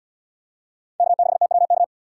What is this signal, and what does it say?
Morse code, heard as a keyed tone, about 77 wpm, 700 Hz. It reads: K6TWK